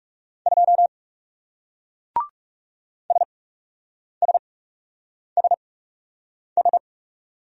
Morse code, heard as a keyed tone, about 38 wpm, spaced as for 8 wpm, 700 Hz. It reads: 2 SSSS